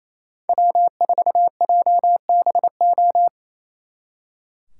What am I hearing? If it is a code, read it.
Morse code, heard as a keyed tone, about 33 wpm, 700 Hz. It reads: W4JBO
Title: Speed Racing - USA 1x3 Call Signs 28wpm